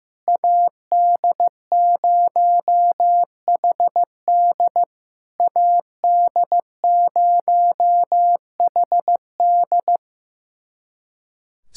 Morse code, heard as a keyed tone, about 15 wpm, 700 Hz. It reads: AD0HD AD0HD
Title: SOTA QSO Practice 15wpm